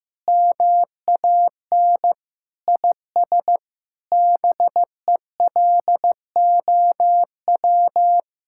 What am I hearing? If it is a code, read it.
Morse code, heard as a keyed tone, about 15 wpm, 700 Hz. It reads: MAN IS BELOW
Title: Sentences from Top 1000 Words - Review Part 5 - 15wpm